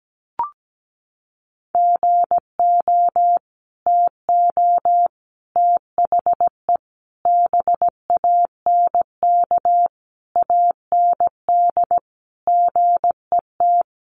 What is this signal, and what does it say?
Morse code, heard as a keyed tone, about 17 wpm, 700 Hz. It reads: GO TO THE BANK AND GET
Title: Sentences from Top 1500 Words 17wpm